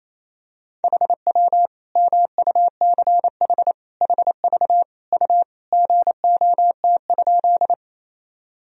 Morse code, heard as a keyed tone, about 28 wpm, 700 Hz. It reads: HW MUCH HV U GOT?
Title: Sentences from Top 300 Words and Abbreviations 28wpm